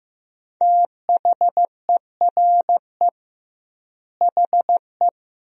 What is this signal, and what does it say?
Morse code, heard as a keyed tone, about 15 wpm, 700 Hz. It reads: THERE HE
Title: 2x Word-Spacing - Sentences from Top 100 Words - 15wpm